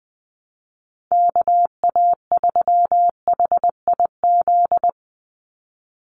Morse code, heard as a keyed tone, about 20 wpm, 700 Hz. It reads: KA3HIZ
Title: USA Call Signs 20wpm